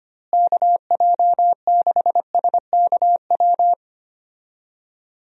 Morse code, heard as a keyed tone, about 25 wpm, 700 Hz. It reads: KJ6SKW
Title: USA Call Signs - 6 Characters - Repeated 3x - 25wpm